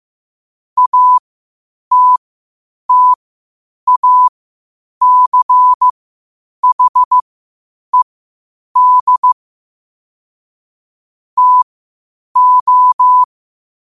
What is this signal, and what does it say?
Morse code, heard as a keyed tone, about 15 wpm, spaced as for 8 wpm, 1000 Hz. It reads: ATTACHED TO
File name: qotdfiles / 8wpm